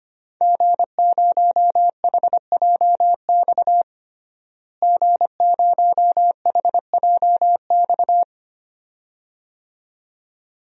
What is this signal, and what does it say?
Morse code, heard as a keyed tone, about 25 wpm, 700 Hz. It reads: G0HJX G0HJX
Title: DXCC-100 Call Signs - 5 Characters - Repeated 2x - 25wpm